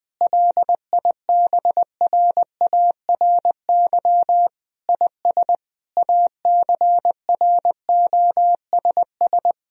Morse code, heard as a keyed tone, about 19 wpm, 700 Hz. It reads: LIBRARY IS ACROSS
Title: Sentences from Top 1000 Words - Review Part 4 - 20wpm